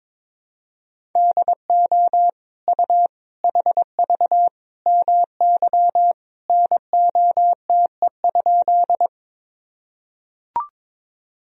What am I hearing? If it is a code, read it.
Morse code, heard as a keyed tone, about 22 wpm, 700 Hz. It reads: DO U HV MY NOTE?
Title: Sentences from Top 500 Words and Abbreviations 22wpm